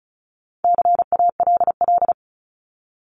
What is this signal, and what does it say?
Morse code, heard as a keyed tone, about 34 wpm, 700 Hz. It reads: CALL
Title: Top 300 Words 35wpm